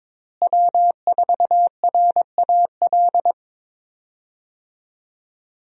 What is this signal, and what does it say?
Morse code, heard as a keyed tone, about 21 wpm, 700 Hz. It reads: W4RAL
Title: USA Call Signs - 5 Characters - Repeated 2x - 22wpm